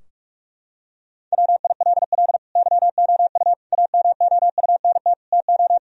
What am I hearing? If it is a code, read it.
Morse code, heard as a keyed tone, about 41 wpm, 700 Hz. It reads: WILL YOU AMOUNT TO
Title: Sentences from Top 1000 Words - Review Part 4 - 45wpm